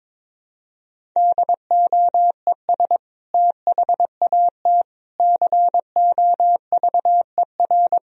Morse code, heard as a keyed tone, about 22 wpm, 700 Hz. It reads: DOES THAT COVER
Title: Sentences from Top 1000 Words - Review Part 1 - 22wpm